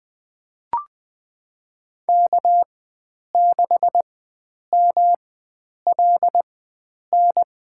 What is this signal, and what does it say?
Morse code, heard as a keyed tone, about 20 wpm, spaced as for 9 wpm, 700 Hz. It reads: K6MLN
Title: USA Call Signs - 5 Characters - 4x Character Spacing - 20wpm